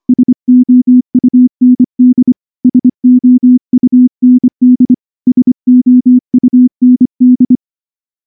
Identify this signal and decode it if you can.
Morse code, heard as a keyed tone, about 23 wpm, 250 Hz. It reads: SOUND SOUND SOUND
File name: morse-training-25WPM